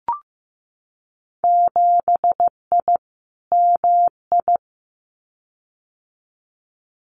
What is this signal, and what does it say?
Morse code, heard as a keyed tone, about 15 wpm, 700 Hz. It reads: 7I MI
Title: Winter Field Day Exchanges Encore 15wpm